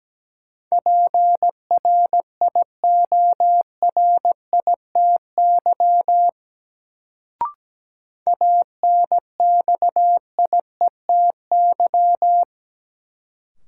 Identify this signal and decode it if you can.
Morse code, heard as a keyed tone, about 17 wpm, 700 Hz. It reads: PRIORITY ANXIETY
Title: Sentences from Top 1600 Words - New Word Drill Alt - 17wpm